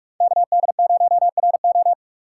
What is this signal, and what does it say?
Morse code, heard as a keyed tone, about 46 wpm, 700 Hz. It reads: KD0RO
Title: USA Call Signs - 5 Characters - 6x Character Spacing - 45wpm